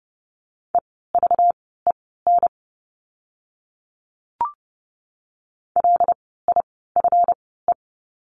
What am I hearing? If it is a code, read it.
Morse code, heard as a keyed tone, about 30 wpm, spaced as for 17 wpm, 700 Hz. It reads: EVEN LIFE
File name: course / 30:10wpm / 230 Lesson 37f - Letter x - Words 30wpm